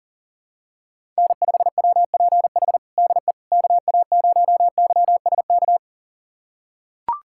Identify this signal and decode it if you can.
Morse code, heard as a keyed tone, about 40 wpm, 700 Hz. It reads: N5WPH DE KA0YSK
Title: USA Callsign DE Callsign 40wpm